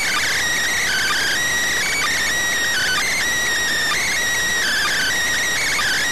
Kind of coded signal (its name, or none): Emergency Alert System (SAME) tones